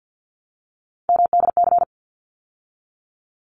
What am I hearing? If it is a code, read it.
Morse code, heard as a keyed tone, about 50 wpm, 700 Hz. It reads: MDC
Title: ARRL Sections 50wpm